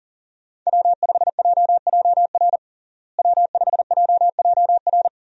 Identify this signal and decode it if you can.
Morse code, heard as a keyed tone, about 40 wpm, 700 Hz. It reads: W5JJR W5JJR
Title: USA Call Signs - 5 Characters - Repeated 3x - 40wpm